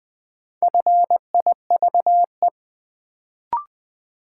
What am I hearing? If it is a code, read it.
Morse code, heard as a keyed tone, about 20 wpm, 700 Hz. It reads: FIVE
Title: Sets of 5 Words - New Word Drill - 20wpm